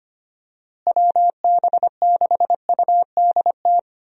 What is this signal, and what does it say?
Morse code, heard as a keyed tone, about 25 wpm, 700 Hz. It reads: WB6UDT
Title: USA Call Signs - 6 Characters - Repeated 3x - 25wpm